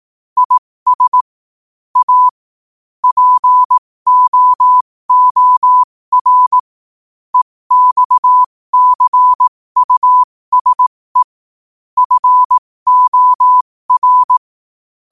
Morse code, heard as a keyed tone, about 18 wpm, spaced as for 15 wpm, 1000 Hz. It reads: IS A POOR EXCUSE FOR
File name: qotdfiles / 15wpm